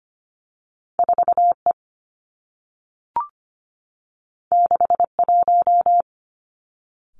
Morse code, heard as a keyed tone, about 25 wpm, 700 Hz. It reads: 4E 61
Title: Two Challenging Letter-Number Combinations 25wpm